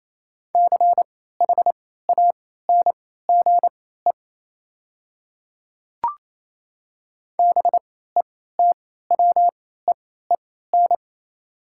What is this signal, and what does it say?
Morse code, heard as a keyed tone, about 28 wpm, spaced as for 16 wpm, 700 Hz. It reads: CHANGE BETWEEN
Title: 3x Character Spacing - Top 300 Words - 28wpm